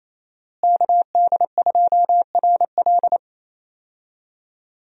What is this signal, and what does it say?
Morse code, heard as a keyed tone, about 28 wpm, 700 Hz. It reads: KD2RL
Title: USA Call Signs - 5 Characters - Repeated 2x - 28wpm